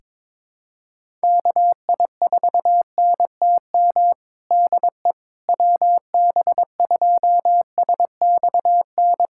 Morse code, heard as a keyed tone, about 22 wpm, 700 Hz. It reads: KI4NTM DE WB2SXN
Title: USA Callsign DE Callsign 22wpm